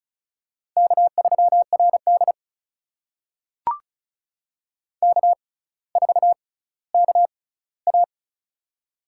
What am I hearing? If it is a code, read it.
Morse code, heard as a keyed tone, about 36 wpm, spaced as for 18 wpm, 700 Hz. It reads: K3RD K4KA
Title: USA Call Signs - 4 Characters - 6x Character Spacing - 35wpm